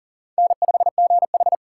Morse code, heard as a keyed tone, about 40 wpm, 700 Hz. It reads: N5GH
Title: USA Call Signs - 4 Characters - 6x Character Spacing - 40wpm